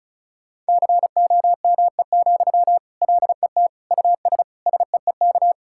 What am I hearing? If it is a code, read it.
Morse code, heard as a keyed tone, about 36 wpm, 700 Hz. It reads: COME, LET US SEEK
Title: Sentences from Top 1600 Words 35wpm